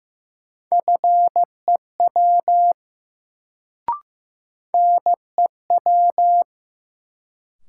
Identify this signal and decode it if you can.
Morse code, heard as a keyed tone, about 15 wpm, 700 Hz. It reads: FEW NEW
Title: Sentences from Top 300 Words - New Word Drill - 15wpm